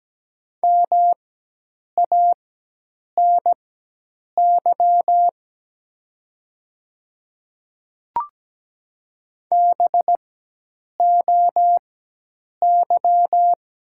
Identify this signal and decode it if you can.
Morse code, heard as a keyed tone, about 17 wpm, spaced as for 8 wpm, 700 Hz. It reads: MANY BOY
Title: 4x Character Spacing - Top 300 Words - 17wpm